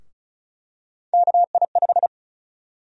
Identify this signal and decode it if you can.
Morse code, heard as a keyed tone, about 34 wpm, 700 Hz. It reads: KI5
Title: USA Call Sign Prefixes - 35wpm